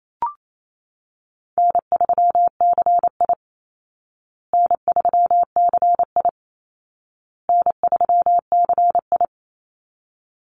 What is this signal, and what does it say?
Morse code, heard as a keyed tone, about 28 wpm, 700 Hz. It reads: N3CI N3CI N3CI
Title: Easy Call Signs Repeated 3x - 28wpm